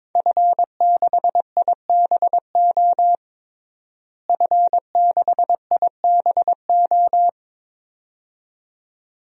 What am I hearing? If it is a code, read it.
Morse code, heard as a keyed tone, about 22 wpm, 700 Hz. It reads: F6IBO F6IBO
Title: DXCC-100 Call Signs - 5 Characters - Repeated 2x - 22wpm